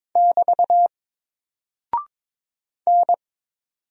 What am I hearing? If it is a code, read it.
Morse code, heard as a keyed tone, about 22 wpm, 700 Hz. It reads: <BT> N
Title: Single Character (Letter:Number:Punctuation:Prosign) 22wpm